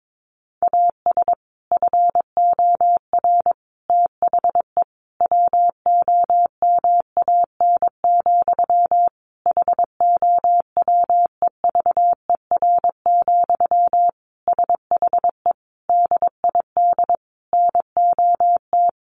Morse code, heard as a keyed tone, about 22 wpm, 700 Hz. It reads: AS FOR THE WOMAN, HOWEVER, SHE DID NOT
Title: Sentences from Top 1100 Words 22wpm